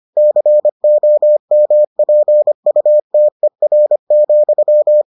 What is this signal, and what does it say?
Morse code, heard as a keyed tone, about 25 wpm, 600 Hz. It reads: COMPUTER,